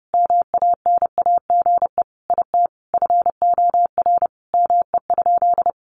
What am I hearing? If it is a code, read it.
Morse code, heard as a keyed tone, about 30 wpm, 700 Hz. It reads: MANAGE IT FOR ME?
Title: Sentences from Top 1400 Words 30wpm